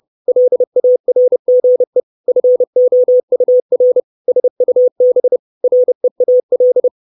Morse code, heard as a keyed tone, about 30 wpm, 500 Hz. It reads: LARGE FOUR SUB REAL
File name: 632 Words 30WPM